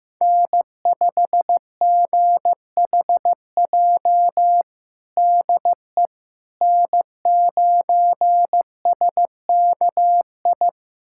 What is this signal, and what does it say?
Morse code, heard as a keyed tone, about 16 wpm, 700 Hz. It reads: N5GHJ DE N9SKI
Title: USA Callsign DE Callsign - 5 Characters - 15wpm